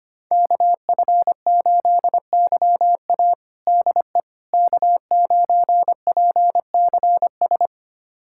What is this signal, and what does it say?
Morse code, heard as a keyed tone, about 25 wpm, 700 Hz. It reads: KF8YA DE K9PCS
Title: USA Callsign DE Callsign - 5 Characters - 25wpm